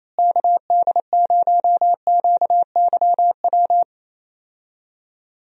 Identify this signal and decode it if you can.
Morse code, heard as a keyed tone, about 28 wpm, 700 Hz. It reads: KD0QYW
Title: USA Call Signs - 6 Characters - Repeated 2x - 28wpm